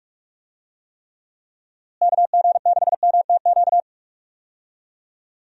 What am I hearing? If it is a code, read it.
Morse code, heard as a keyed tone, about 46 wpm, 700 Hz. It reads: KG6MTQ
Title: USA Call Signs - 6 Characters - Repeated 3x - 45wpm